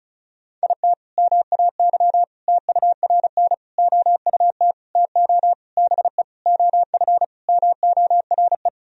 Morse code, heard as a keyed tone, about 34 wpm, 700 Hz. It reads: IT MAY TURN OUT TO BE OF MORE
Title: Sentences from Top 400 Words and Abbreviations 35wpm